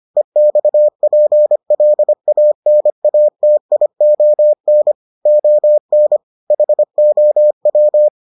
Morse code, heard as a keyed tone, about 25 wpm, 600 Hz. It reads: EXPLANATION ON HOW